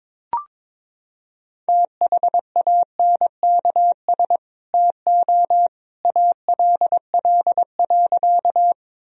Morse code, heard as a keyed tone, about 21 wpm, 700 Hz. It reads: THANKS TO ALL.
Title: Sentences from Top 1000 Words - Review Part 3 - 22wpm